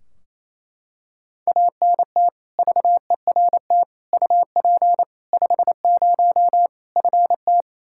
Morse code, heard as a keyed tone, about 28 wpm, 700 Hz. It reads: ANT VERT UP 50 FT
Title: Practice QSO - Exchange 2 - Info - 28wpm